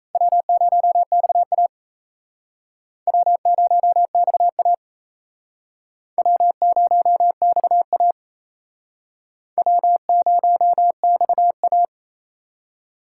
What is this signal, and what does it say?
Morse code, heard as a keyed tone, about 37 wpm, 700 Hz. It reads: W0XA W0XA W0XA W0XA